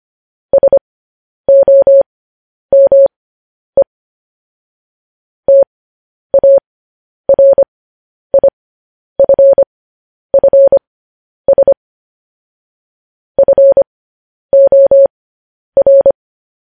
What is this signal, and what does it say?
Morse code, heard as a keyed tone, about 25 wpm, spaced as for 10 wpm, 550 Hz. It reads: SOME TARIFFS FOR
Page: news headlines in morse code at 10 wpm